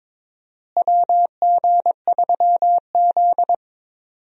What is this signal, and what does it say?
Morse code, heard as a keyed tone, about 22 wpm, 700 Hz. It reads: WG3Z
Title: USA Call Signs - 4 Characters - Repeated 2x - 22wpm